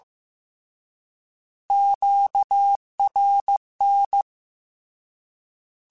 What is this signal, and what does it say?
Morse code, heard as a keyed tone, about 15 wpm, 800 Hz. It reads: QRN
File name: Misc / Misc-mp3s / Words @13